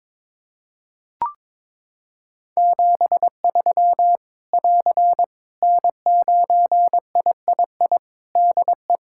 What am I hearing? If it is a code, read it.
Morse code, heard as a keyed tone, about 21 wpm, 700 Hz. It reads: 73 <AR> N9III DE
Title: Practice QSO - Exchange 4 - Ending - 22wpm